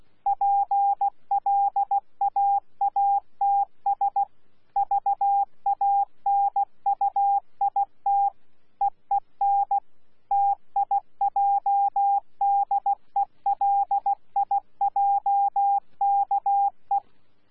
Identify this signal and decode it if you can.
Morse code, heard as a keyed tone, about 16 wpm, 800 Hz. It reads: PLAATS VANUIT EEN TIJDELIJKE
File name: Z PI4VRZ / PI4VRZ 14-jun-2025